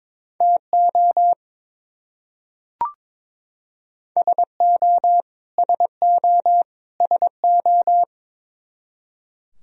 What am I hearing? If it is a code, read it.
Morse code, heard as a keyed tone, about 22 wpm, 700 Hz. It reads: TO SO SO SO
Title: Top 100 Words Repeated 22wpm